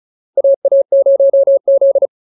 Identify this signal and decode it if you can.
Morse code, heard as a keyed tone, about 35 wpm, 550 Hz. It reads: AA0Z